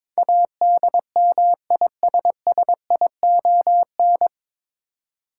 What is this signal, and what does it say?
Morse code, heard as a keyed tone, about 22 wpm, 700 Hz. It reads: ADMISSION
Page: Morse Code Ninja practice 22wpm